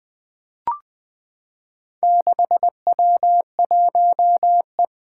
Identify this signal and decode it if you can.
Morse code, heard as a keyed tone, about 20 wpm, 700 Hz. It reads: 6W1E
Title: Four Challenging Letter-Number Combinations 20wpm